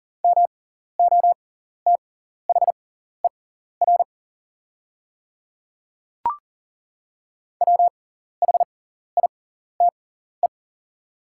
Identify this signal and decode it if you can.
Morse code, heard as a keyed tone, about 38 wpm, spaced as for 14 wpm, 700 Hz. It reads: MOTHER WHITE
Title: 6x Character Spacing - Top 500 Words - 40wpm